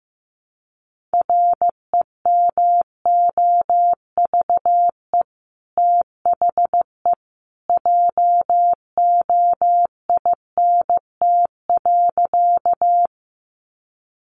Morse code, heard as a keyed tone, about 15 wpm, 700 Hz. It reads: REMOVE THE JOINT.
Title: Sentences from Top 1300 Words 15wpm